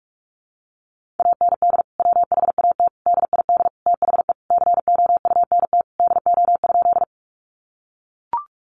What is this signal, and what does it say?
Morse code, heard as a keyed tone, about 44 wpm, 700 Hz. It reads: AND WHAT DID THE COUNT DO?
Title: Sentences from Top 1500 Words 45wpm